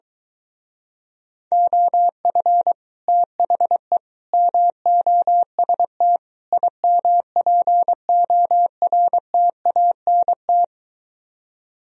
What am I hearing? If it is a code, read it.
Morse code, heard as a keyed tone, about 23 wpm, 700 Hz. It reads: OF THE MOST IMPORTANT